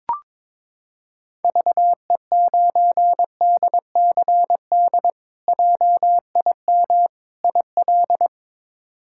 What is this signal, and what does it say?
Morse code, heard as a keyed tone, about 22 wpm, 700 Hz. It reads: VE9DCD JIM IL
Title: K1USN SST QSO Practice - 22wpm